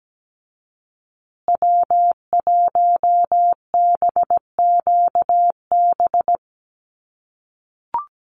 Morse code, heard as a keyed tone, about 17 wpm, 700 Hz. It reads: W1BQB
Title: USA Call Signs Repeated 2x 17wpm